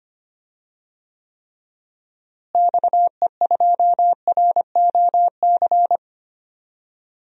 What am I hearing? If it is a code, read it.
Morse code, heard as a keyed tone, about 25 wpm, 700 Hz. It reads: XE2ROC
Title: DXCC-100 Call Signs - 6 Characters - Repeated 2x - 25wpm